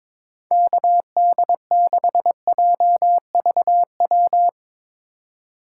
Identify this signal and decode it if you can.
Morse code, heard as a keyed tone, about 22 wpm, 700 Hz. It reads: KD6JVW
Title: USA Call Signs - 6 Characters - 8x Character Spacing - 22wpm